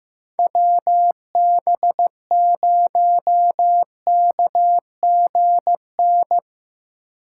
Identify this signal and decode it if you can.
Morse code, heard as a keyed tone, about 17 wpm, 700 Hz. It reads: WB0KGN
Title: Speed Racing - USA Call Signs 15wpm